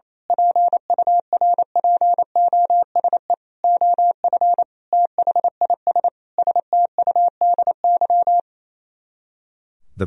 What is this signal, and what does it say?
Morse code, heard as a keyed tone, about 27 wpm, 700 Hz. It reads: PURPOSE OF THIS STUDY
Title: Speed-Racing - Sets of 5 Words 28wpm